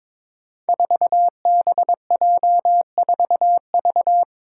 Morse code, heard as a keyed tone, about 23 wpm, 700 Hz. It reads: 4BJ4V
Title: Five Challenging Letter-Number Combinations 22wpm